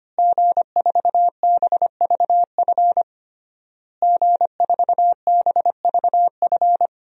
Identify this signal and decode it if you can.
Morse code, heard as a keyed tone, about 25 wpm, 700 Hz. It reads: G4BVF G4BVF
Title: DXCC-100 Call Signs - 5 Characters - Repeated 3x - 25wpm